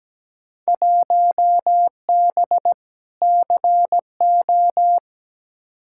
Morse code, heard as a keyed tone, about 17 wpm, 700 Hz. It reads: 1B CO
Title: Field Day Exchanges - 17wpm